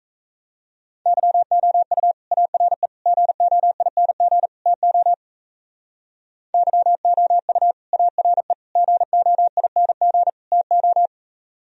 Morse code, heard as a keyed tone, about 35 wpm, 700 Hz. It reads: YOU ARE GOING TO YOU ARE GOING TO
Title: Speed-Racing - Sets of 4 Words 28wpm